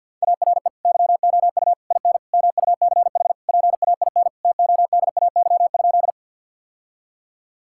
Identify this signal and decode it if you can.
Morse code, heard as a keyed tone, about 50 wpm, 700 Hz. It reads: ARE YOU IN MUCH PAIN TODAY?
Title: Sentences from Top 1600 Words 50wpm